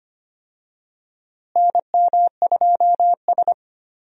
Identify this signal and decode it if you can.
Morse code, heard as a keyed tone, about 25 wpm, 700 Hz. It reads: NM2S
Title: USA Call Signs - 4 Characters - Repeated 3x - 25wpm